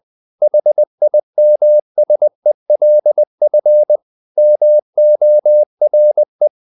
Morse code, heard as a keyed tone, about 20 wpm, 600 Hz. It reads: HIMSELF MORE
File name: the old man and the sea / the old man and the sea part 1 20wpm